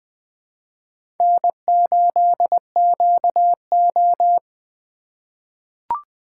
Morse code, heard as a keyed tone, about 20 wpm, 700 Hz. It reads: N8QO
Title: USA Call Signs - 4 Characters - 4x Character Spacing - 20wpm